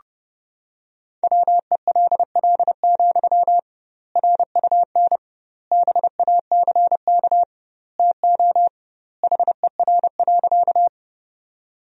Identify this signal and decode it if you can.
Morse code, heard as a keyed tone, about 30 wpm, 700 Hz. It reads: WELL, RUN BACK TO HER.
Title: 2x Word-Spacing - Sentences from Top 300 Words - 30wpm